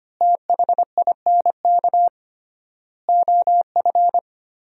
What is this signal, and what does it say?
Morse code, heard as a keyed tone, about 25 wpm, 700 Hz. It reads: THINK OF
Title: 3x Word-Spacing - Sentences from Top 300 Words - 25wpm